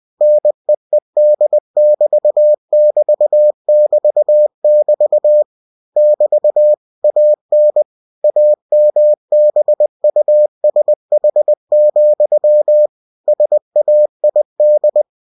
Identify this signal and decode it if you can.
Morse code, heard as a keyed tone, about 20 wpm, 600 Hz. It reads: NEED<BT><BT><BT><BT> <BT> AN AMBUSH, SAID
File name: cw-books / winnie-the-pooh / winnie-20wpm-page-0082